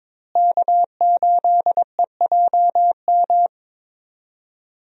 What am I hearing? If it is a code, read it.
Morse code, heard as a keyed tone, about 22 wpm, 700 Hz. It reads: K8EJM
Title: USA Call Signs - 5 Characters - Repeated 2x - 22wpm